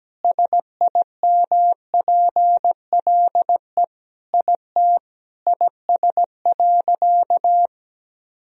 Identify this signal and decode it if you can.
Morse code, heard as a keyed tone, about 17 wpm, 700 Hz. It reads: SIMPLE IT IS.
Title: Sentences from Top 400 Words and Abbreviations 17wpm